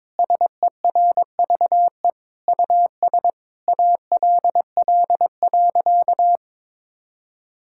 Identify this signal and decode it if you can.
Morse code, heard as a keyed tone, about 22 wpm, 700 Hz. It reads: SERVE US ALL.
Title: Sentences from Top 1000 Words - Review Part 5 - 22wpm